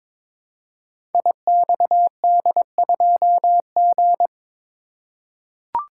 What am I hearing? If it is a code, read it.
Morse code, heard as a keyed tone, about 22 wpm, 700 Hz. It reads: IXD2G
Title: Code Groups - 5 Letter:Numbers - 22wpm